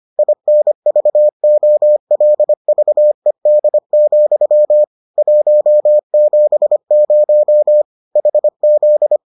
Morse code, heard as a keyed tone, about 25 wpm, 600 Hz. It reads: INVOLVED, 170 HZ